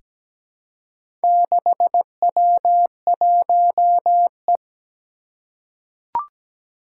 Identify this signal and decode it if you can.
Morse code, heard as a keyed tone, about 17 wpm, 700 Hz. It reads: 6W1E
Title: Four Challenging Letter-Number Combinations 17wpm